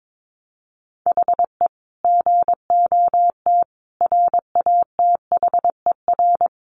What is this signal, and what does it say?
Morse code, heard as a keyed tone, about 22 wpm, 700 Hz. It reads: HE GOT RATHER
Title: Sentences from Top 700 Words 22wpm